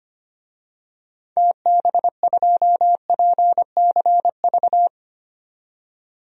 Morse code, heard as a keyed tone, about 25 wpm, 700 Hz. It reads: TB2PCV
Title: DXCC-100 Call Signs - 6 Characters - 25wpm